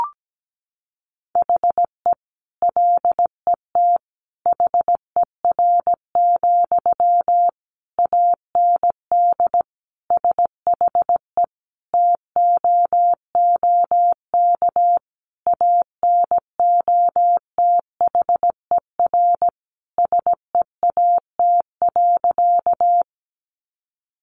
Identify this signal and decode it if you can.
Morse code, heard as a keyed tone, about 17 wpm, 700 Hz. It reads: HE LET HER, AND SHE TOOK ANOTHER SEAT.
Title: Sentences from Top 1500 Words 17wpm